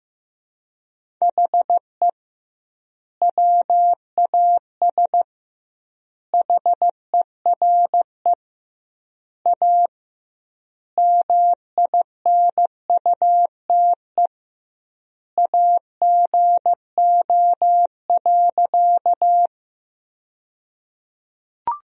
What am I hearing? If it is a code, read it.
Morse code, heard as a keyed tone, about 16 wpm, 700 Hz. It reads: HE WAS HERE A MINUTE AGO.
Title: 2x Word-Spacing - Sentences from Top 500 Words - 15wpm